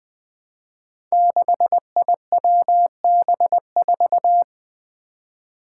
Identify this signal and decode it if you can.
Morse code, heard as a keyed tone, about 20 wpm, 700 Hz. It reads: 6IWB4
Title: Five Challenging Letter-Number Combinations 20wpm